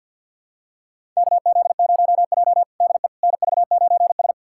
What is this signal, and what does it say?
Morse code, heard as a keyed tone, about 51 wpm, 700 Hz. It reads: KZ0J DE NV9S
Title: USA Callsign DE Callsign - 4 Characters - 50wpm